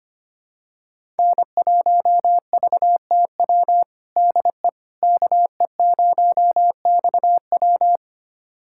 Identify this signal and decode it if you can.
Morse code, heard as a keyed tone, about 25 wpm, 700 Hz. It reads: N1VTW DE KE0XW
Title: USA Callsign DE Callsign - 5 Characters - 25wpm